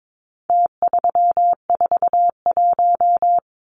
Morse code, heard as a keyed tone, about 22 wpm, 700 Hz. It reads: T341
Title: Numbers - Contest Style - 4-Digit 22wpm